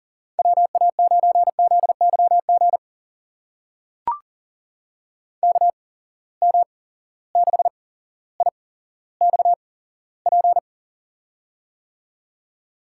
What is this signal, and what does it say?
Morse code, heard as a keyed tone, about 40 wpm, spaced as for 17 wpm, 700 Hz. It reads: WA9ZYG KM6IXP
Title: USA Call Signs - 6 Characters - 8x Character Spacing - 40wpm